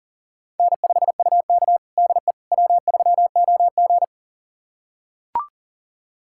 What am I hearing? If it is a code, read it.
Morse code, heard as a keyed tone, about 37 wpm, 700 Hz. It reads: N5UK DE W3OG
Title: USA Callsign DE Callsign - 4 Characters - 40wpm